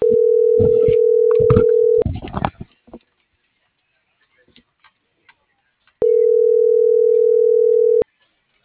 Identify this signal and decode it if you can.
Ringback tone (North America), 440 + 480 Hz.